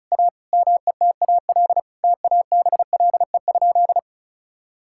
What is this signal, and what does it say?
Morse code, heard as a keyed tone, about 36 wpm, 700 Hz. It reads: A METAL TABLE?
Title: Sentences from Top 1000 Words - Review Part 6 - 35wpm